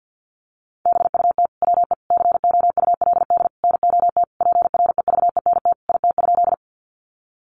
Morse code, heard as a keyed tone, about 50 wpm, 700 Hz. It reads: BUT WE COULD NOT PREVENT IT?
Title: Sentences from Top 1100 Words 50wpm